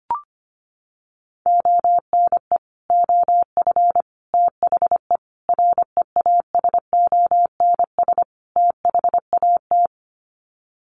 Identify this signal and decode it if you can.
Morse code, heard as a keyed tone, about 25 wpm, 700 Hz. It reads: ONE OF THE REASONS THAT
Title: Sets of 5 Words Encore 25wpm